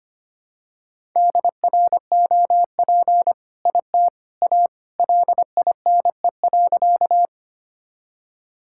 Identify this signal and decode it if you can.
Morse code, heard as a keyed tone, about 25 wpm, 700 Hz. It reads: DROP IT A LINE.
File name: practice / 25wpm / All-Age Appropriate Jokes - 25wpm